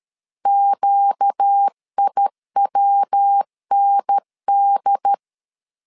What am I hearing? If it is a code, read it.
Morse code, heard as a keyed tone, about 13 wpm, 800 Hz. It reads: QIWND
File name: LEZIONI / 4 Q 12wpm 1833